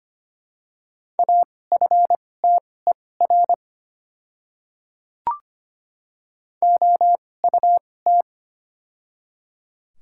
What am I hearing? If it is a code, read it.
Morse code, heard as a keyed tone, about 25 wpm, spaced as for 18 wpm, 700 Hz. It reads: AFTER OUT
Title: 2x Character Spacing - Top 500 Words - 25wpm